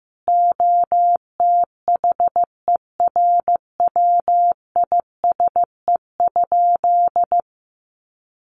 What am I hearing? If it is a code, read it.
Morse code, heard as a keyed tone, about 15 wpm, 700 Hz. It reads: OTHERWISE?
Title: Sentences from Top 1200 Words 15wpm